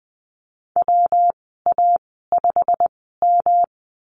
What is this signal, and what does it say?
Morse code, heard as a keyed tone, about 20 wpm, spaced as for 15 wpm, 700 Hz. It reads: WA5M
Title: 173 Lesson 28g - Number 9 - Call Signs 20wpm